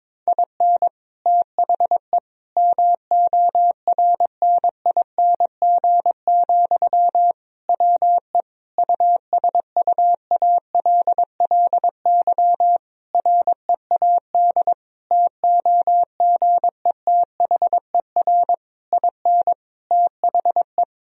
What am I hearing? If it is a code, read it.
Morse code, heard as a keyed tone, about 22 wpm, 700 Hz. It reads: IN THE MORNING, WE USUALLY READ TOGETHER IN THE
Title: Sentences from Top 1000 Words - Review Part 3 - 22wpm